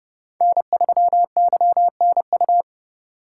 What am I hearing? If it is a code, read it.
Morse code, heard as a keyed tone, about 30 wpm, 700 Hz. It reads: N3YNU